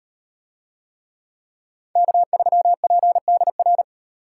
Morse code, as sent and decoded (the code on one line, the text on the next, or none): -.- ...-- .--. -.. .-.
K3PDR